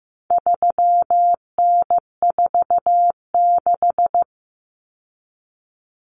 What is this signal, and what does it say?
Morse code, heard as a keyed tone, about 15 wpm, 700 Hz. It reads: 3N46
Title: Common Cut Numbers - 4-Digit 15wpm